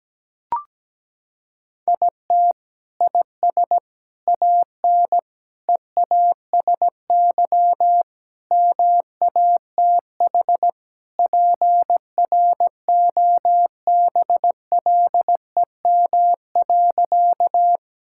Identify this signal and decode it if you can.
Morse code, heard as a keyed tone, about 17 wpm, 700 Hz. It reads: IT IS AN EASY MATH PROBLEM.
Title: Sentences from Top 1000 Words - Review Part 6 - 17wpm